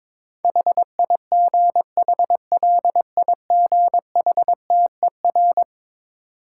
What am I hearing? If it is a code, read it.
Morse code, heard as a keyed tone, about 22 wpm, 690 Hz. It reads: HIGHLIGHTER
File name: Difficult Words with Hints - 22wpm